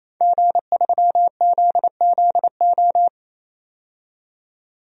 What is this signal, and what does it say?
Morse code, heard as a keyed tone, about 28 wpm, 700 Hz. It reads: G3ZZO
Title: DXCC-100 Call Signs - 5 Characters - 28wpm